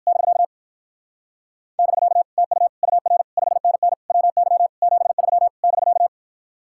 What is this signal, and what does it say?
Morse code, heard as a keyed tone, about 58 wpm, 700 Hz. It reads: <BK> <BK> TU UR 5NN WY 73 <BK>